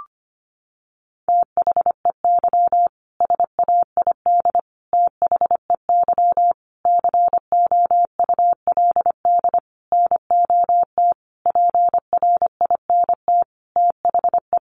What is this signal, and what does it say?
Morse code, heard as a keyed tone, about 25 wpm, 700 Hz. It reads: THEY SAID THEY COULD NOT PRINT THE
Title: Sentences from Top 1400 Words 25wpm